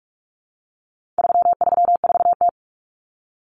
Morse code, heard as a keyed tone, about 44 wpm, 700 Hz. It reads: 334T
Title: Common Cut Numbers - 4-Digit 45wpm